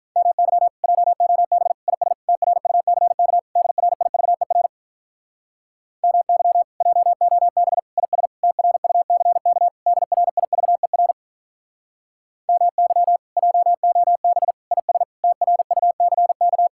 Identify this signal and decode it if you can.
Morse code, heard as a keyed tone, about 44 wpm, 700 Hz. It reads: MY JOB IS TRUCK DRIVER MY JOB IS TRUCK DRIVER MY JOB IS TRUCK